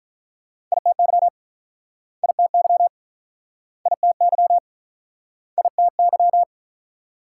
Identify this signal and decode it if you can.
Morse code, heard as a keyed tone, about 49 wpm, 700 Hz. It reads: ITY ITY ITY ITY